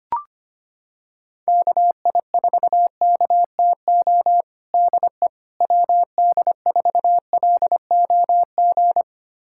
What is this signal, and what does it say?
Morse code, heard as a keyed tone, about 25 wpm, 700 Hz. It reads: KI4KTO DE WD4LOG
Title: USA Callsign DE Callsign - 6 Characters - 25wpm